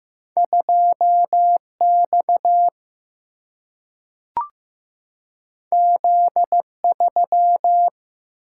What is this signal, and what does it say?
Morse code, heard as a keyed tone, about 15 wpm, 700 Hz. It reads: 2 X Z 3
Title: Single-Letter Single-Number 15wpm